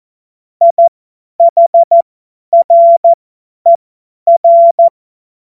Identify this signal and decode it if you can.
Morse code, heard as a keyed tone, about 14 wpm, spaced as for 10 wpm, 680 Hz. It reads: IHRER
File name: Morsekurs / Witze / Witz 31. Januar 2023 50 F70